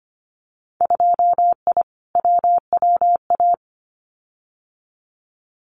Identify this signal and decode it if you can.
Morse code, heard as a keyed tone, about 25 wpm, 700 Hz. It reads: 2I WWA
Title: Winter Field Day Exchanges Encore 25wpm